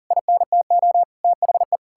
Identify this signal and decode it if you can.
Morse code, heard as a keyed tone, about 37 wpm, 700 Hz. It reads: INTO THE
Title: Sentences from Top 1600 Words 40wpm